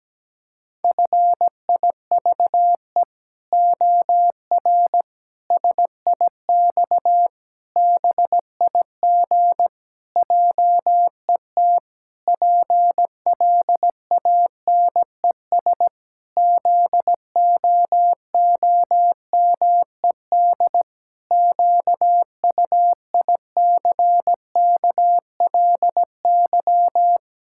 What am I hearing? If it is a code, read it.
Morse code, heard as a keyed tone, about 17 wpm, 700 Hz. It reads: FIVE OR SIX BIG JET PLANES ZOOMED QUICKLY
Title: Pangram Challenge v2 - 17wpm